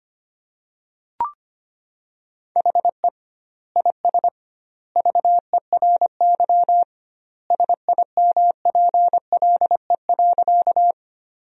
Morse code, heard as a keyed tone, about 25 wpm, 700 Hz. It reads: HE IS VERY SIMPLE.
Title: 2x Word-Spacing - Sentences from Top 400 Words - 25wpm